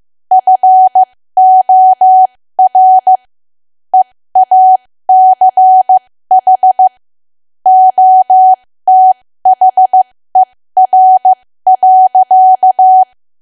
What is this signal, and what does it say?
Morse code, heard as a keyed tone, about 15 wpm, spaced as for 13 wpm, 750 Hz. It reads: FOR EACH OTHER.
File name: cwa / W1AW-Level2 / 13WPM